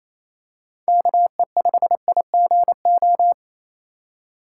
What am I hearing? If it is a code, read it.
Morse code, heard as a keyed tone, about 28 wpm, 700 Hz. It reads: KE5IGO
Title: USA Call Signs - 6 Characters - 4x Character Spacing - 28wpm